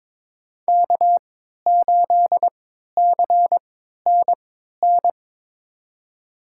Morse code, heard as a keyed tone, about 21 wpm, spaced as for 12 wpm, 700 Hz. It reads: K8CNN
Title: USA Call Signs - 5 Characters - 3x Character Spacing - 22wpm